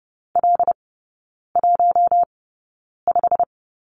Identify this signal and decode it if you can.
Morse code, heard as a keyed tone, about 30 wpm, spaced as for 9 wpm, 700 Hz. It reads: L15
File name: Lesson 18e - Letter m - Three Characters 30wpm